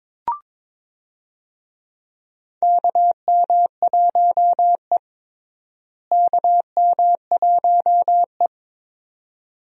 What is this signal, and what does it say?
Morse code, heard as a keyed tone, about 22 wpm, 700 Hz. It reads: KM1E KM1E
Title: USA Call Signs - 4 Characters - Repeated 2x - 22wpm